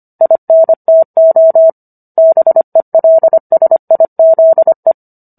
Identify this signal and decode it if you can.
Morse code, heard as a keyed tone, about 25 wpm, 650 Hz. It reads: INTO BELSIZE
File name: Book One / 16-Chapter 16 - 25 WPM